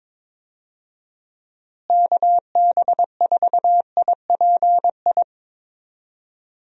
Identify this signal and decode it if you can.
Morse code, heard as a keyed tone, about 22 wpm, 700 Hz. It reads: KB4IPI
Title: USA Call Signs - 6 Characters - Repeated 2x - 22wpm